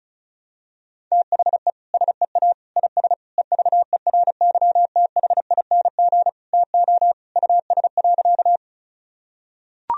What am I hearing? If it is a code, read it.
Morse code, heard as a keyed tone, about 34 wpm, 700 Hz. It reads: THE SEA IS EVERYTHING TO US.
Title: Sentences from Top 1000 Words - Review Part 1 - 35wpm